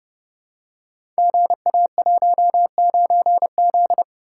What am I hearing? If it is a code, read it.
Morse code, heard as a keyed tone, about 30 wpm, 700 Hz. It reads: GA19Z
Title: Code Groups - 5 Letter:Numbers - 30wpm